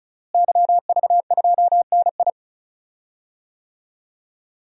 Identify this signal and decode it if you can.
Morse code, heard as a keyed tone, about 34 wpm, 700 Hz. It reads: YV2NI
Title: DXCC-100 Call Signs - 5 Characters - Repeated 2x - 35wpm